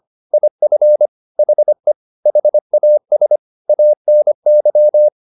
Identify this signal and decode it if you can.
Morse code, heard as a keyed tone, about 25 wpm, 600 Hz. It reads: IF HE HAS ANY
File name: the old man and the sea part 1 25wpm